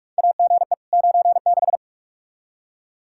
Morse code, heard as a keyed tone, about 47 wpm, 700 Hz. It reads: AGE 96
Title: Practice QSO - Exchange 3 - More Info - 45wpm